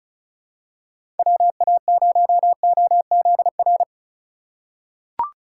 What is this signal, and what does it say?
Morse code, heard as a keyed tone, about 36 wpm, 700 Hz. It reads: WA0OZR
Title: USA Call Signs - 6 Characters - 2x Character Spacing - 35wpm